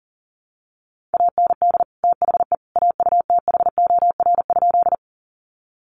Morse code, heard as a keyed tone, about 40 wpm, 700 Hz. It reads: AND THE AUTHOR?
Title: Sentences from Top 1200 Words 40wpm